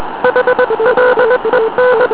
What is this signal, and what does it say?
Morse code, heard as a keyed tone, about 20 wpm, 500 Hz, with noise. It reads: HLEN